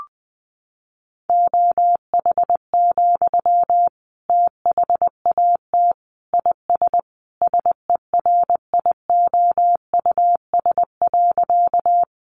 Morse code, heard as a keyed tone, about 20 wpm, 700 Hz. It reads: OH, THAT IS SERIOUS.
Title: Sentences from Top 1100 Words 20wpm